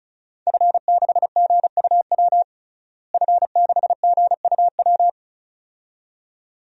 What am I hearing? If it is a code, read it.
Morse code, heard as a keyed tone, about 34 wpm, 700 Hz. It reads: F6GUW F6GUW
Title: DXCC-100 Call Signs - 5 Characters - Repeated 2x - 35wpm